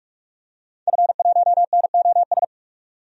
Morse code, heard as a keyed tone, about 46 wpm, 700 Hz. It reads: F1NOS
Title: DXCC-100 Call Signs - 5 Characters - 45wpm